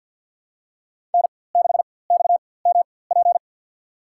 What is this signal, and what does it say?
Morse code, heard as a keyed tone, about 52 wpm, spaced as for 24 wpm, 700 Hz. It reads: N6XMP
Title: USA Call Signs - 4x Character Spacing - 50wpm